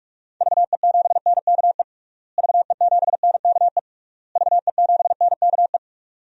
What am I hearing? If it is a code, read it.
Morse code, heard as a keyed tone, about 46 wpm, 700 Hz. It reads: VE7NKE VE7NKE VE7NKE
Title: DXCC-100 Call Signs - 6 Characters - Repeated 3x - 45wpm